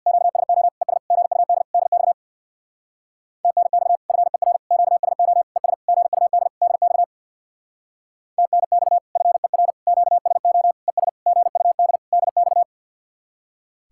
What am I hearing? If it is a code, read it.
Morse code, heard as a keyed tone, about 62 wpm, 700 Hz. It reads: QSO ES GUD DX TNX FER QSO ES GUD DX TNX FER QSO ES GUD DX
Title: Speed-Racing Common QSO Phrases Encore - 50wpm